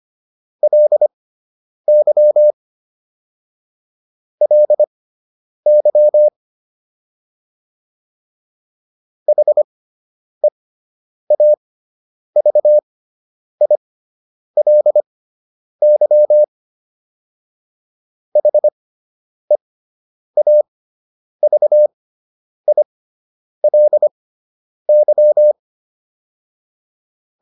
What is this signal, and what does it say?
Morse code, heard as a keyed tone, about 25 wpm, spaced as for 9 wpm, 600 Hz. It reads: LY LY HEAVILY HEAVILY
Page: fundamental-practice-files